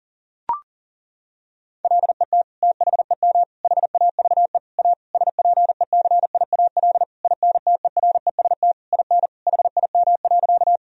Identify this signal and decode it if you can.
Morse code, heard as a keyed tone, about 37 wpm, 700 Hz. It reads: LET THEM HAVE A SPECIAL INTEREST IN HIM.
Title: Sentences from Top 1000 Words - Review Part 3 - 40wpm